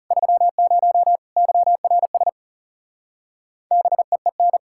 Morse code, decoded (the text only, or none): 30 YRS BEEN